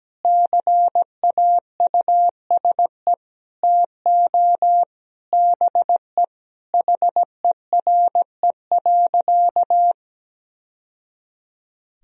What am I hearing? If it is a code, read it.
Morse code, heard as a keyed tone, about 17 wpm, 700 Hz. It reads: CAUSE TO BE HERE.
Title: Sentences from Top 1000 Words - Review Part 1 - 17wpm